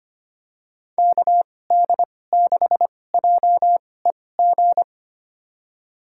Morse code, heard as a keyed tone, about 25 wpm, spaced as for 18 wpm, 700 Hz. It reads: KD6JEG
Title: USA Call Signs - 6 Characters - 2x Character Spacing - 25wpm